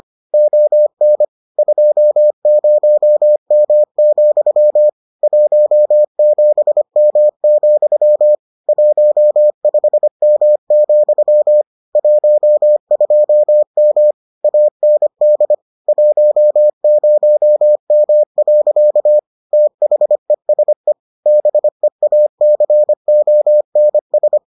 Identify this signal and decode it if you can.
Morse code, heard as a keyed tone, about 25 wpm, 600 Hz. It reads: ON 20M, 17M, 15M, 12M AND 10M. THESE BEACONS